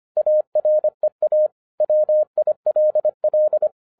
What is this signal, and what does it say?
Morse code, heard as a keyed tone, about 25 wpm, 610 Hz. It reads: AREA WILL
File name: 2011-07-mp3 / n39 THE 3.11 2 25wpm